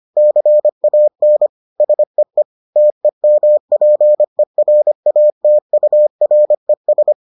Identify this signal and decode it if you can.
Morse code, heard as a keyed tone, about 25 wpm, 600 Hz. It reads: CAN SEE TEMPERATURES